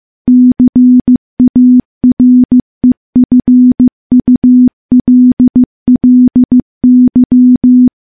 Morse code, heard as a keyed tone, about 15 wpm, 250 Hz. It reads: CAREFULLY
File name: morse / morse-training-15WPM